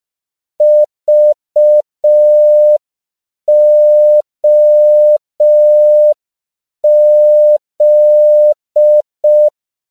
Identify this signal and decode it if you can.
Morse code, heard as a keyed tone, about 5 wpm, 600 Hz. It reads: VOZ
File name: CW / 5PPM